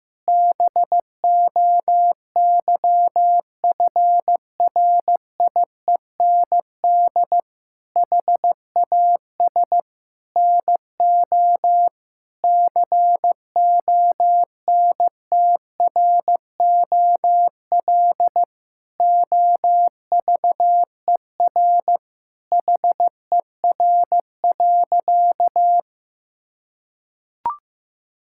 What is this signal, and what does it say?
Morse code, heard as a keyed tone, about 15 wpm, 700 Hz. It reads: BOYFRIEND HAS NO CONTROL OVER HER.
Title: Sentences from Top 1000 Words - Review Part 5 - 15wpm